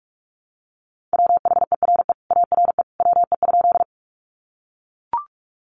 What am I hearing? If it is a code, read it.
Morse code, heard as a keyed tone, about 44 wpm, 700 Hz. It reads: WHERE ARE WE?
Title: Sentences from Top 200 Words 45wpm